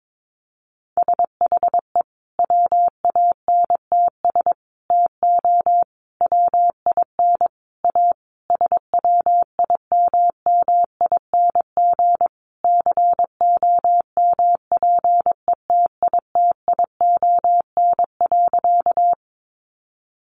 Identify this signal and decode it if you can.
Morse code, heard as a keyed tone, about 22 wpm, 700 Hz. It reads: SHE WANTS TO WIN A SWIMMING COMPETITION.
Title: Additional Easy Phrases 22wpm